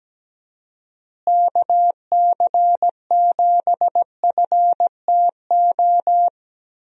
Morse code, heard as a keyed tone, about 17 wpm, 700 Hz. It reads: KC7FTO
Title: USA Call Signs - 6 Characters - Repeated 2x - 17wpm